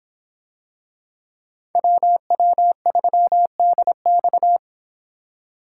Morse code, heard as a keyed tone, about 26 wpm, 700 Hz. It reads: WW3DX